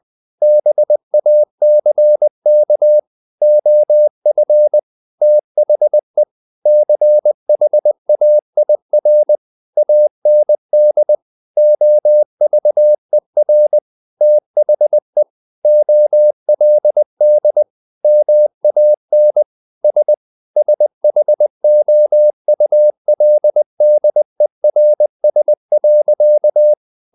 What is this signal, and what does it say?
Morse code, heard as a keyed tone, about 20 wpm, 600 Hz. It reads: BACK OF THE CHAIR AND OVER THE OLD MAN S SHOULDERS.
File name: the old man and the sea / the old man and the sea part 1 20wpm